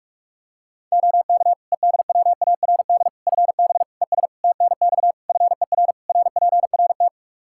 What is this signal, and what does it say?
Morse code, heard as a keyed tone, about 41 wpm, 700 Hz. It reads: OK EDWARD FB ES TNX FER RPRT
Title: Practice QSO - Exchange 2 - Info - 45wpm